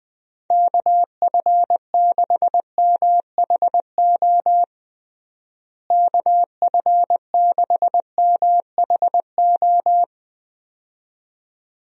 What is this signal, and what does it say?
Morse code, heard as a keyed tone, about 20 wpm, 700 Hz. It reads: KF6MHO KF6MHO
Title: USA Call Signs - 6 Characters - Repeated 2x - 20wpm